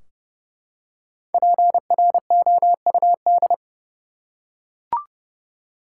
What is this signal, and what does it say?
Morse code, heard as a keyed tone, about 30 wpm, 700 Hz. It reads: PROUD
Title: Five Letter Words - 30wpm